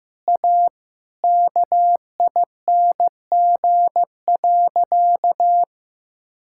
Morse code, heard as a keyed tone, about 15 wpm, 700 Hz. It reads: A KING.
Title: Sentences from Top 1000 Words - Review Part 5 - 15wpm